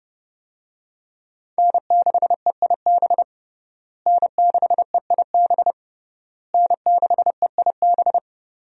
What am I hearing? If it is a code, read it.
Morse code, heard as a keyed tone, about 30 wpm, 700 Hz. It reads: N6EIB N6EIB N6EIB
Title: USA Call Signs - 5 Characters - Repeated 3x - 30wpm